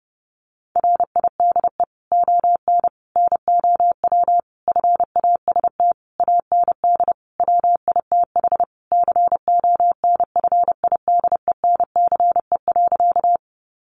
Morse code, heard as a keyed tone, about 30 wpm, 700 Hz. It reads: RIDE ON NOW FAST AND WITH CONFIDENCE.
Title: Sentences from Top 1500 Words 30wpm